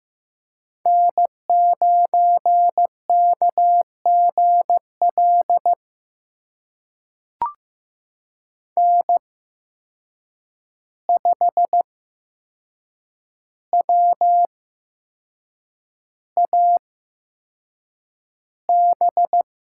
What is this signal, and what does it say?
Morse code, heard as a keyed tone, about 15 wpm, spaced as for 6 wpm, 700 Hz. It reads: N9KGL N5WAB
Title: USA Call Signs - 5 Characters - 8x Character Spacing - 15wpm